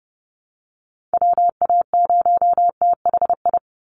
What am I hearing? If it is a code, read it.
Morse code, heard as a keyed tone, about 30 wpm, 700 Hz. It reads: WA0THI
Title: USA Call Signs 30wpm